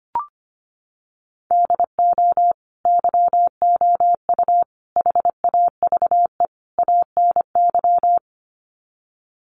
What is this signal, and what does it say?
Morse code, heard as a keyed tone, about 25 wpm, 700 Hz. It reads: DO YOU HAVE ANY
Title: Sets of 4 Words 25wpm